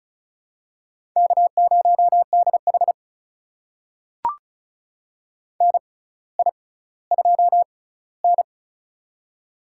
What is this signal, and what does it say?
Morse code, heard as a keyed tone, about 36 wpm, spaced as for 18 wpm, 700 Hz. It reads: K0DH NI2N
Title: USA Call Signs - 4 Characters - 6x Character Spacing - 35wpm